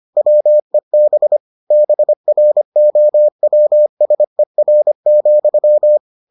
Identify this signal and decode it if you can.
Morse code, heard as a keyed tone, about 25 wpm, 600 Hz. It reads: WEB BROWSER,